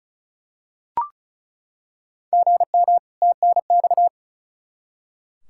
Morse code, heard as a keyed tone, about 36 wpm, 700 Hz. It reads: GM TNX
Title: Practice QSO - Exchange 1 - Beginning - 35wpm